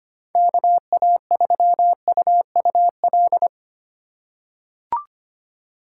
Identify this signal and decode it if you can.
Morse code, heard as a keyed tone, about 25 wpm, 700 Hz. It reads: KA3UUL
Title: USA Call Signs - 6 Characters - 3x Character Spacing - 25wpm